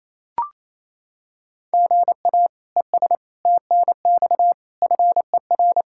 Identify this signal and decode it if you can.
Morse code, heard as a keyed tone, about 28 wpm, 700 Hz. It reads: GA ES TNX FER
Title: Practice QSO - Exchange 1 - Beginning - 28wpm